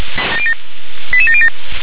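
Five-tone selective call (ZVEI), heard as quick runs of five tones, tones of about 70 ms. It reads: 88707
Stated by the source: Five-tone ZVEI selective call, received by SDR.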